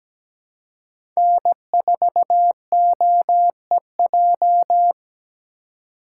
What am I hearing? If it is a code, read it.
Morse code, heard as a keyed tone, about 17 wpm, 700 Hz. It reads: N4OEJ
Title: USA Call Signs - 5 Characters - 3x Character Spacing - 17wpm